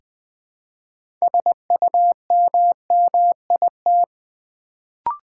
Morse code, heard as a keyed tone, about 20 wpm, 700 Hz. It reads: SUMMIT
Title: Build A Word - 2 Letters At A Time - 6-Letter Words - 20wpm